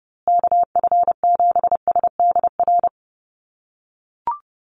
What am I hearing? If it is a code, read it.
Morse code, heard as a keyed tone, about 30 wpm, 700 Hz. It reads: KF7SDR
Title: USA Call Signs Repeated 2x 30wpm